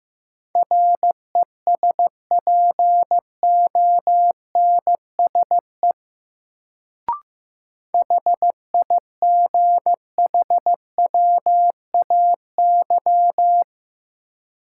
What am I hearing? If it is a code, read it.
Morse code, heard as a keyed tone, about 15 wpm, 700 Hz. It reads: RESPONSE HIGHWAY
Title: Sentences from Top 1400 Words - New Word Drill - 15wpm